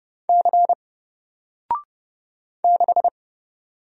Morse code, heard as a keyed tone, about 30 wpm, 700 Hz. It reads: C 6
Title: Single Character (Letter:Number:Punctuation:Prosign) 30wpm